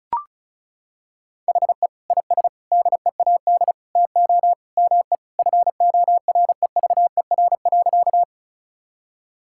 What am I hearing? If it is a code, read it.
Morse code, heard as a keyed tone, about 36 wpm, 700 Hz. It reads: HE IS DEAD TO ME FOREVER.
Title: Sentences from Top 1600 Words 35wpm